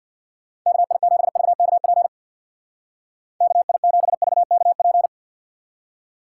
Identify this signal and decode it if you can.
Morse code, heard as a keyed tone, about 64 wpm, 700 Hz. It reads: KI7VKP KI7VKP